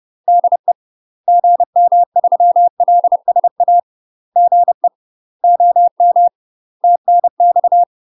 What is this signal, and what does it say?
Morse code, heard as a keyed tone, about 30 wpm, 700 Hz. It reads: DE GM3LSA GE OM TNX
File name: UK MASTER QSOs TRAINING / 30WPM / UK MASTER QSOs 0018